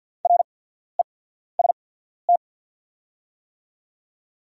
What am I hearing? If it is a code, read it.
Morse code, heard as a keyed tone, about 48 wpm, spaced as for 14 wpm, 690 Hz. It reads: REST
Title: 8x Character Spacing - Top 500 Words - 50wpm